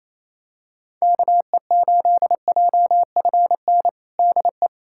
Morse code, heard as a keyed tone, about 28 wpm, 700 Hz. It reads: KE8JFN DE
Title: USA Callsign DE Callsign - 6 Characters - 28wpm